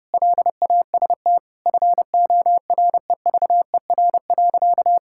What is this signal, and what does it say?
Morse code, heard as a keyed tone, about 30 wpm, 700 Hz. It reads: LAST FOREVER.
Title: Sentences from Top 1600 Words 30wpm